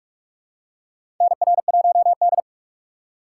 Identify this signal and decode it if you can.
Morse code, heard as a keyed tone, about 42 wpm, 700 Hz. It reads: NR1D
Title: USA Call Signs - 4 Characters - 2x Character Spacing - 45wpm